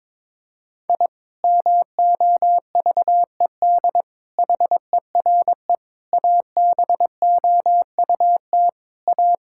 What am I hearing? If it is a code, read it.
Morse code, heard as a keyed tone, about 22 wpm, 700 Hz. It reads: I MOVED HERE ABOUT A
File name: Sentences with 7-Word Phrases - 22wpm